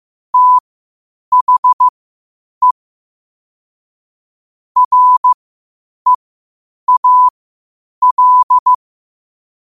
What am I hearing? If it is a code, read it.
Morse code, heard as a keyed tone, about 15 wpm, spaced as for 8 wpm, 1000 Hz. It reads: THE REAL
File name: qotdfiles / 8wpm